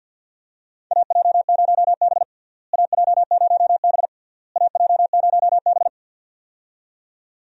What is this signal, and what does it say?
Morse code, heard as a keyed tone, about 52 wpm, 700 Hz. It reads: AJ0B AJ0B AJ0B
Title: USA Call Signs - 4 Characters - Repeated 3x - 50wpm